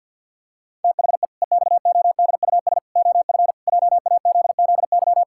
Morse code, heard as a keyed tone, about 50 wpm, 700 Hz. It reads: THE EXODUS OF JAZZY
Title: Pangram Challenge v2 - 50wpm